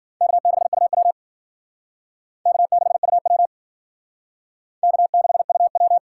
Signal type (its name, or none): Morse code